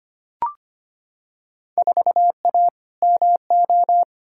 Morse code, heard as a keyed tone, about 25 wpm, 700 Hz. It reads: 4A MO
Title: Field Day Exchanges - 25wpm